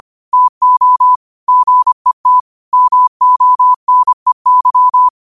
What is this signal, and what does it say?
Morse code, heard as a keyed tone, about 25 wpm, 1000 Hz. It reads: TO GET MONEY
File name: qotdfiles / 25wpm